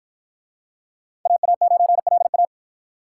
Morse code, heard as a keyed tone, about 47 wpm, 700 Hz. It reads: AA9LA